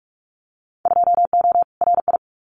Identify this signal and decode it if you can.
Morse code, heard as a keyed tone, about 44 wpm, 700 Hz. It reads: 2O RI
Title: Winter Field Day Exchanges 45wpm